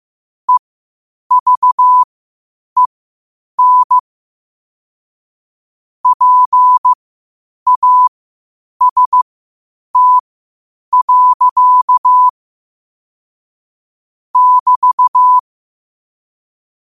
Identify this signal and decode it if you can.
Morse code, heard as a keyed tone, about 15 wpm, spaced as for 8 wpm, 1000 Hz. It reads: EVEN PAST. <BT>